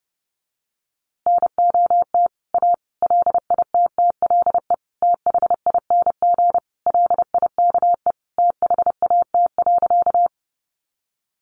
Morse code, heard as a keyed tone, about 30 wpm, 700 Hz. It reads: NOT A LITTLE THING LIKE THAT.
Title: A Sound Of Thunder 30wpm